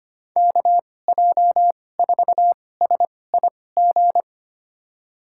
Morse code, heard as a keyed tone, about 25 wpm, spaced as for 18 wpm, 700 Hz. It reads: KJ4SIG
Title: USA Call Signs - 6 Characters - 2x Character Spacing - 25wpm